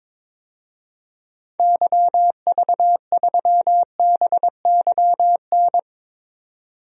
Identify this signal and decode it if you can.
Morse code, heard as a keyed tone, about 22 wpm, 700 Hz. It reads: YV3BYN
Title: DXCC-100 Call Signs - 6 Characters - 22wpm